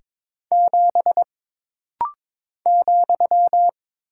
Morse code, heard as a keyed tone, about 22 wpm, 700 Hz. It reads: 7 ,
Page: This rapid-fire timing is geared to push your ICR (Instant Character Recognition) skills.